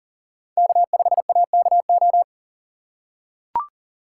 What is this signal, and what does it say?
Morse code, heard as a keyed tone, about 40 wpm, 700 Hz. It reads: K5AKO
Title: USA Call Signs - 5 Characters - 6x Character Spacing - 40wpm